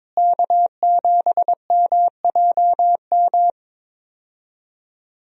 Morse code, heard as a keyed tone, about 22 wpm, 700 Hz. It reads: K7MJM
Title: USA Call Signs - 5 Characters - Repeated 3x - 22wpm